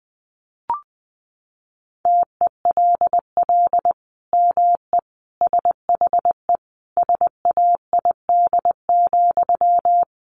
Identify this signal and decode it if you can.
Morse code, heard as a keyed tone, about 20 wpm, 700 Hz. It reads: TELL ME SHE SAID,
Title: Sentences from Top 1200 Words 20wpm